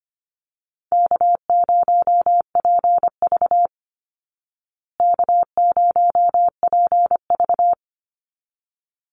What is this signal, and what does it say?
Morse code, heard as a keyed tone, about 25 wpm, 700 Hz. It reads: K0PV K0PV
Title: Easy Call Signs Repeated 2x - 25wpm